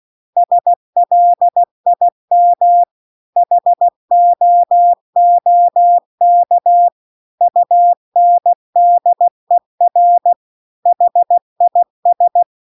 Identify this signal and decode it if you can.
Morse code, heard as a keyed tone, about 16 wpm, 700 Hz. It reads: SLIM HOOK UNDER HIS